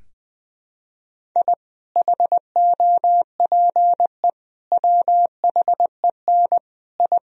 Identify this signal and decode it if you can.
Morse code, heard as a keyed tone, about 20 wpm, 700 Hz. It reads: I HOPE WHEN I
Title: Sentences from Top 1000 Words - Review Part 1 - 20wpm